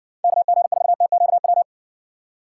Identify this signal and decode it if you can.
Morse code, heard as a keyed tone, about 59 wpm, 700 Hz. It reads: KG4TQW